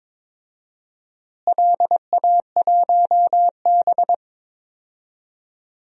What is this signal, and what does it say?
Morse code, heard as a keyed tone, about 22 wpm, 700 Hz. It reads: LA1B
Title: DXCC-100 Call Signs - 4 Characters - Repeated 2x - 22wpm